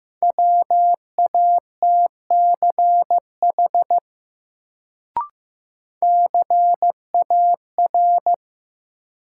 Morse code, heard as a keyed tone, about 15 wpm, 700 Hz. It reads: WATCH CAR
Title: Sentences from Top 300 Words - New Word Drill - 15wpm